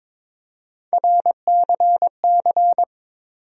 Morse code, heard as a keyed tone, about 22 wpm, 700 Hz. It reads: RCC
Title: USA Call Sign Postfixes - 3 Letters 22wpm